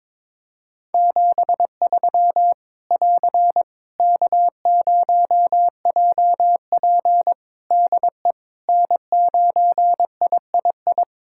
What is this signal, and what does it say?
Morse code, heard as a keyed tone, about 22 wpm, 700 Hz. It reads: 73 <AR> K0JP DE N9III
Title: Practice QSO - Exchange 4 - Ending - 22wpm